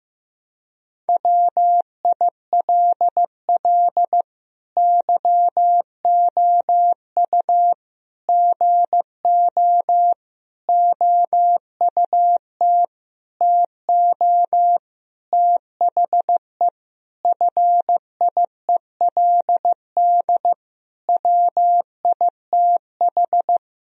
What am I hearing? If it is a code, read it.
Morse code, heard as a keyed tone, about 15 wpm, 700 Hz. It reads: WILL YOU GO OUT TO THE FIELD WITH
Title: Sentences from Top 1000 Words - Review Part 2 - 15wpm